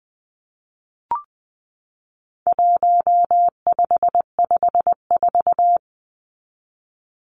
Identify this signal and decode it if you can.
Morse code, heard as a keyed tone, about 20 wpm, 700 Hz. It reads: 1554
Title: Numbers - 4-Digit 20wpm